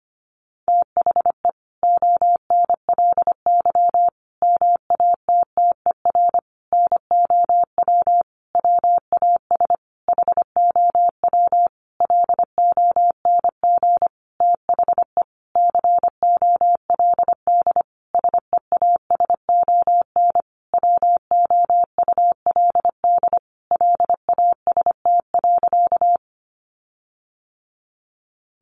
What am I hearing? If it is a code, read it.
Morse code, heard as a keyed tone, about 25 wpm, 700 Hz. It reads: THE ONLY MATTER NOW WAS HOW LONG THE COLD SEASON WOULD LAST.
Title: Sentences from Top 1400 Words 25wpm